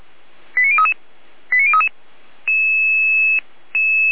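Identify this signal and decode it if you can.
Five-tone selective call (ZVEI), heard as quick runs of five tones, tones of about 70 ms. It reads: 89033 89033
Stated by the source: ZVEI five-tone call 89125, received on SDR.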